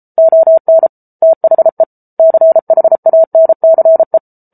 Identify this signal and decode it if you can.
Morse code, heard as a keyed tone, about 34 wpm, 650 Hz. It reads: ON THE CHANCE